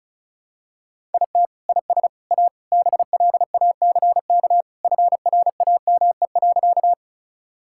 Morse code, heard as a keyed tone, about 36 wpm, 700 Hz. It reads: IT IS A BLACK FRAME.
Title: Sentences from Top 1000 Words - Review Part 6 - 35wpm